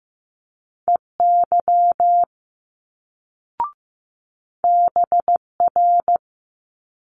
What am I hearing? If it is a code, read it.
Morse code, heard as a keyed tone, about 15 wpm, 700 Hz. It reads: EY BR
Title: Multi-letter Phonemes Quick Fire 15wpm